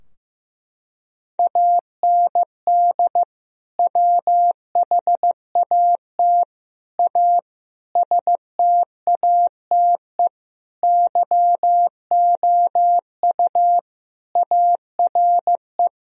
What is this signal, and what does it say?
Morse code, heard as a keyed tone, about 15 wpm, 700 Hz. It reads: AND WHAT A STATE YOU ARE
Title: Sentences from Top 1000 Words - Review Part 2 - 15wpm